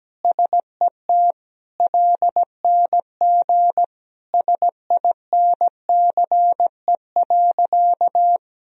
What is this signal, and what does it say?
Morse code, heard as a keyed tone, about 17 wpm, 700 Hz. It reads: SET LNG SINCE.
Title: Sentences from Top 300 Words and Abbreviations 17wpm